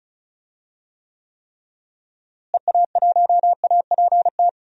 Morse code, heard as a keyed tone, about 34 wpm, 700 Hz. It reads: EA1APT
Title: DXCC-100 Call Signs - 6 Characters - Repeated 3x - 35wpm